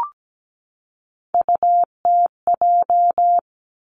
Morse code, heard as a keyed tone, about 17 wpm, 700 Hz. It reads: UTJ
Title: Three Challenging Letter-Number Combinations 17wpm